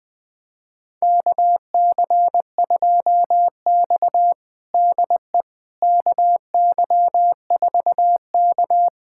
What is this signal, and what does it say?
Morse code, heard as a keyed tone, about 21 wpm, 700 Hz. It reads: KC2X DE KY4K
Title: USA Callsign DE Callsign - 4 Characters - 20wpm